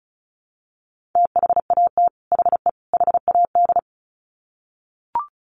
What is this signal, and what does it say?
Morse code, heard as a keyed tone, about 34 wpm, 700 Hz. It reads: THAT HE HAD
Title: Sets of 3 Words 35wpm